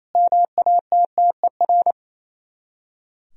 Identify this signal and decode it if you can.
Morse code, heard as a keyed tone, about 35 wpm, 700 Hz. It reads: MATTER
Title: Speed-Racing - Sets of 4 Words 28wpm